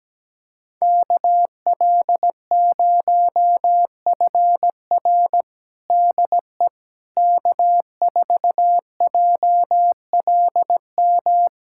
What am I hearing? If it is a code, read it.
Morse code, heard as a keyed tone, about 17 wpm, 700 Hz. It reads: KL0FR DE K4JLM
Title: USA Callsign DE Callsign - 5 Characters - 17wpm